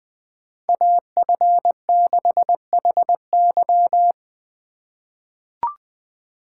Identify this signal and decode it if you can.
Morse code, heard as a keyed tone, about 20 wpm, 700 Hz. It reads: AF6HY
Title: USA Call Signs - 5 Characters - 4x Character Spacing - 20wpm